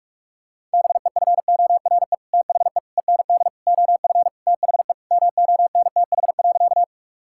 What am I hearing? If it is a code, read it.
Morse code, heard as a keyed tone, about 45 wpm, 700 Hz. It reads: BEFORE THE END OF THE MONTH.
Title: Sentences from Top 1000 Words - Review Part 5 - 45wpm